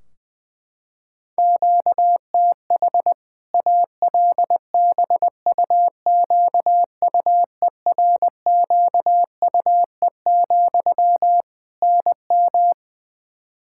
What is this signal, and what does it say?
Morse code, heard as a keyed tone, about 20 wpm, 700 Hz. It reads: QTH ALBUQUERQUE, NM
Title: Practice QSO - Exchange 1 - Beginning - 20wpm